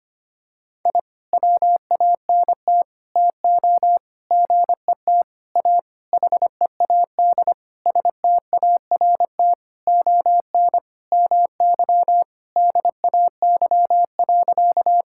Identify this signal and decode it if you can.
Morse code, heard as a keyed tone, about 25 wpm, 700 Hz. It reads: I WANT TO GET A HEAD START ON MY DAY.
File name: practice / 25wpm / Sentences with 9-Word Phrases - 25wpm